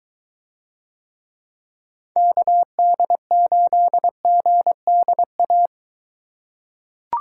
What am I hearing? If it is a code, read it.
Morse code, heard as a keyed tone, about 22 wpm, 700 Hz. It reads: KD8GDA